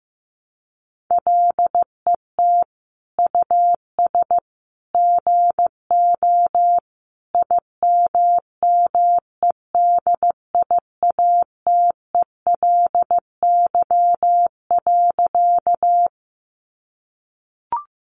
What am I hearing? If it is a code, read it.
Morse code, heard as a keyed tone, about 15 wpm, 700 Hz. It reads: LET US GO IMMEDIATELY.
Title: Sentences from Top 1100 Words 15wpm